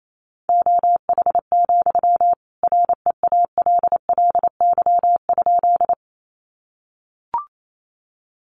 Morse code, heard as a keyed tone, about 28 wpm, 700 Hz. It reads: OH, REALLY?
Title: Sentences from Top 600 Words 28wpm